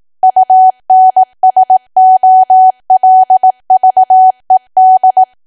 Morse code, heard as a keyed tone, about 18 wpm, 750 Hz. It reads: UNSOLVED
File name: Archive / 18 WPM / 18WPM